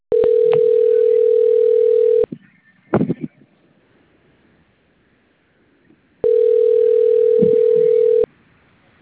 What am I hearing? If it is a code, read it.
Ringback tone (North America), 440 + 480 Hz.
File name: out- - - - - .33699